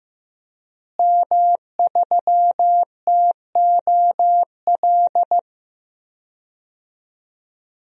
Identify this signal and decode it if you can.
Morse code, heard as a keyed tone, about 15 wpm, 700 Hz. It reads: M3TOL
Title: DXCC-100 Call Signs - 5 Characters - Repeated 3x - 15wpm